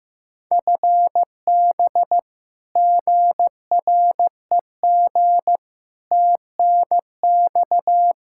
Morse code, heard as a keyed tone, about 15 wpm, 700 Hz. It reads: FB GREG TNX
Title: Practice QSO - Exchange 3 - More Info - 15wpm